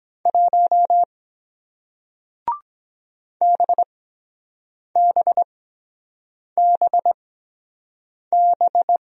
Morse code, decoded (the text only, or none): 1 BBBB